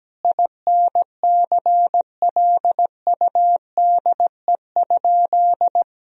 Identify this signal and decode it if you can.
Morse code, heard as a keyed tone, about 17 wpm, 700 Hz. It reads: INCLUDE?
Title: Sentences from Top 1000 Words - Review Part 3 - 17wpm